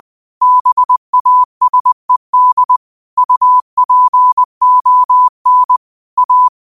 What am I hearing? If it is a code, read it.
Morse code, heard as a keyed tone, about 20 wpm, 1000 Hz. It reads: BASED UPON A
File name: qotdfiles / 20wpm